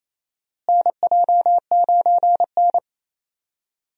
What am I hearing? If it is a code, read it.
Morse code, heard as a keyed tone, about 28 wpm, 700 Hz. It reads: NJ9N
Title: USA Call Signs - 4 Characters - Repeated 2x - 28wpm